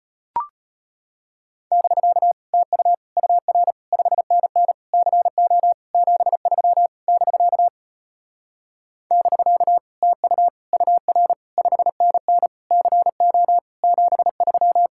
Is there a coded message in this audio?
Yes: Morse code from 1.7 to 14.9 s, heard as a keyed tone, about 33 wpm, 700 Hz. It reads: <BK> TU UR 5NN CO 73 <BK> <BK> TU UR 5NN CO 73